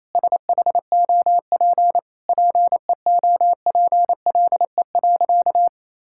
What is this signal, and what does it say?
Morse code, heard as a keyed tone, about 28 wpm, 700 Hz. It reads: SHOP PEOPLE.
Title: Sentences from Top 1600 Words 28wpm